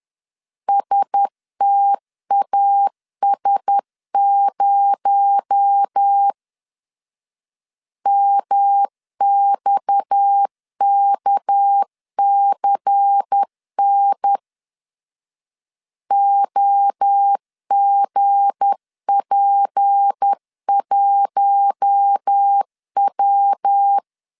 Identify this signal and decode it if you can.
Morse code, heard as a keyed tone, about 10 wpm, 800 Hz. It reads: STAS0 MXKCN OGP1W
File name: CW / DET / 1 9 1500C S A 10wpm 4511